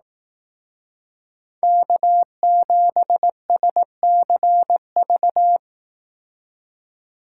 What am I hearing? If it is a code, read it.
Morse code, heard as a keyed tone, about 18 wpm, 700 Hz. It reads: K7SCV
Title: Speed Racing - USA 1x3 Call Signs 15wpm